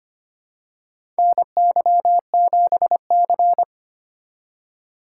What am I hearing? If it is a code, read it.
Morse code, heard as a keyed tone, about 25 wpm, 700 Hz. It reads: NY7C
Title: USA Call Signs - 4 Characters - 4x Character Spacing - 25wpm